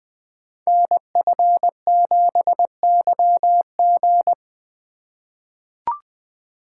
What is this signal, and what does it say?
Morse code, heard as a keyed tone, about 20 wpm, 700 Hz. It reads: NF7YG
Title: Code Groups - 5 Letter:Numbers - 20wpm